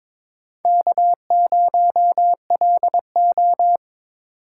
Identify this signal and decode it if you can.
Morse code, heard as a keyed tone, about 22 wpm, 700 Hz. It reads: K0LO
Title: USA Call Signs - 6x Character Spacing - 22wpm